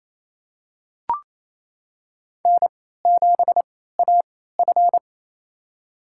Morse code, heard as a keyed tone, about 28 wpm, spaced as for 16 wpm, 700 Hz. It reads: N7AF
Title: USA Call Signs - 3x Character Spacing - 28wpm